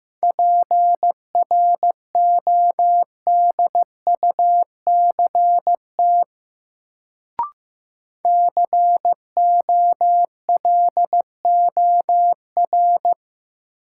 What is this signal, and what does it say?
Morse code, heard as a keyed tone, about 15 wpm, 700 Hz. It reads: PRODUCT COLOR
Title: Sentences from Top 1000 Words - Review Part 2 - New Word Drill - 15wpm